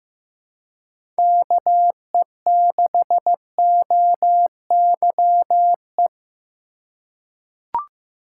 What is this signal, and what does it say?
Morse code, heard as a keyed tone, about 15 wpm, 700 Hz. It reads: KE6OYE
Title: USA Call Signs - 4x Character Spacing - 15wpm